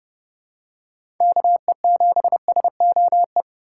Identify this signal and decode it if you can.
Morse code, heard as a keyed tone, about 31 wpm, 700 Hz. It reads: KE7SOE
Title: USA Call Signs - 6 Characters - 6x Character Spacing - 30wpm